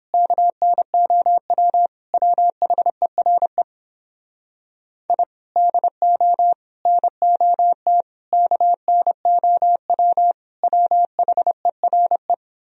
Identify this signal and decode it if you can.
Morse code, heard as a keyed tone, about 29 wpm, 700 Hz. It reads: KNOW WHERE I DO NOT KNOW WHERE